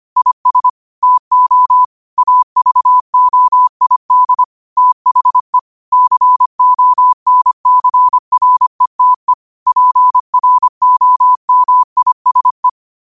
Morse code, heard as a keyed tone, about 25 wpm, 1000 Hz. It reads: IS TO AVOID THE CONCRETE PROMISE